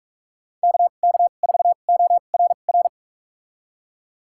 Morse code, heard as a keyed tone, about 46 wpm, spaced as for 33 wpm, 700 Hz. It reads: KK4ORR
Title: USA Call Signs - 6 Characters - 2x Character Spacing - 45wpm